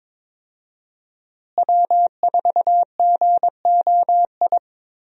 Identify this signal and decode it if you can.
Morse code, heard as a keyed tone, about 21 wpm, 700 Hz. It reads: W4GOI
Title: USA Call Signs - 5 Characters - Repeated 3x - 22wpm